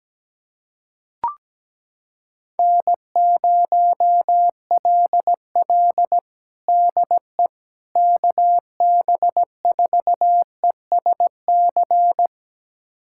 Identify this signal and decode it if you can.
Morse code, heard as a keyed tone, about 17 wpm, 700 Hz. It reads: N0LL DE KB4ESC
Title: USA Callsign DE Callsign 17wpm